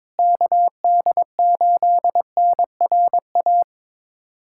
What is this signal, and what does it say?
Morse code, heard as a keyed tone, about 22 wpm, 700 Hz. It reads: KD8NRA
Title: USA Callsign DE Callsign 22wpm